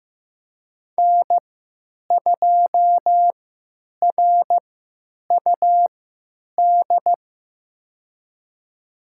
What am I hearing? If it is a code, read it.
Morse code, heard as a keyed tone, about 15 wpm, spaced as for 9 wpm, 700 Hz. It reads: N2RUD
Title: USA Call Signs - 3x Character Spacing - 15wpm